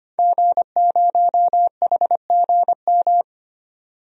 Morse code, heard as a keyed tone, about 25 wpm, 700 Hz. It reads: G0HGM
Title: DXCC-100 Call Signs - 5 Characters - 25wpm